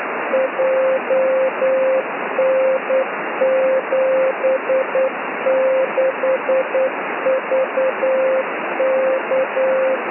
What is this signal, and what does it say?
Morse code, heard as a keyed tone, about 9 wpm, 530 Hz, with radio noise. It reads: JN76VK
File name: s55zmb beacon